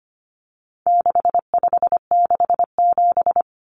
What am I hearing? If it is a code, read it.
Morse code, heard as a keyed tone, about 25 wpm, 700 Hz. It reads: 6567
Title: Numbers - 4-Digit 25wpm